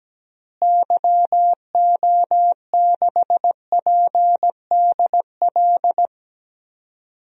Morse code, heard as a keyed tone, about 17 wpm, 700 Hz. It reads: YO6PDL
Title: DXCC-100 Call Signs - 6 Characters - Repeated 2x - 17wpm